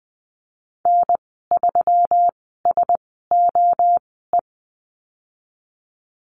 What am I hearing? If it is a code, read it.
Morse code, heard as a keyed tone, about 20 wpm, spaced as for 15 wpm, 700 Hz. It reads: N3SOE
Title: 122 Lesson 20g - Number 3 - Call Signs 20wpm